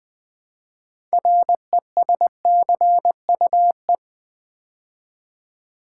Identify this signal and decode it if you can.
Morse code, heard as a keyed tone, about 20 wpm, 700 Hz. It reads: RESCUE
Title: Build A Word - 3 Letters At A Time - 6-Letter Words - 20wpm